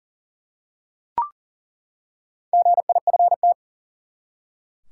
Morse code, heard as a keyed tone, about 40 wpm, 700 Hz. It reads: GIFT
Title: Four Letter Words - 40wpm